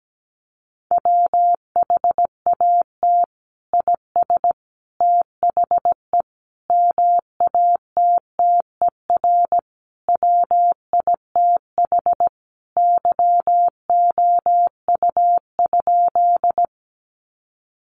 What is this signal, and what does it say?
Morse code, heard as a keyed tone, about 17 wpm, 700 Hz. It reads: WHAT IS THE MATTER WITH YOU?
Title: Sentences from Top 700 Words 17wpm